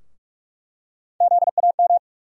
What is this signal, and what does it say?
Morse code, heard as a keyed tone, about 46 wpm, 700 Hz. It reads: ZAM
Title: USA Call Sign Postfixes - 3 Letters 45wpm